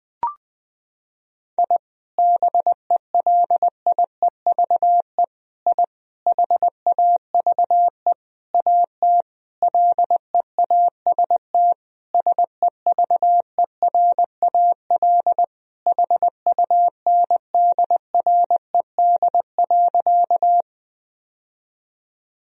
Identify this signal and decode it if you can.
Morse code, heard as a keyed tone, about 20 wpm, 700 Hz. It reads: I BELIEVE I HAVE AT LEAST SEVERAL HUNDRED.
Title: Sentences from Top 1000 Words - Review Part 2 - 20wpm